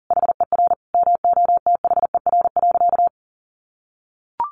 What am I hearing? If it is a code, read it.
Morse code, heard as a keyed tone, about 40 wpm, 700 Hz. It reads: HER MOTHER.
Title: Sentences from Top 1300 Words 40wpm